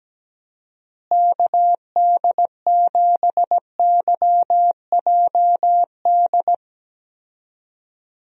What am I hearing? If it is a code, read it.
Morse code, heard as a keyed tone, about 17 wpm, 700 Hz. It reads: KD7YJD
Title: USA Call Signs - 6 Characters - Repeated 2x - 17wpm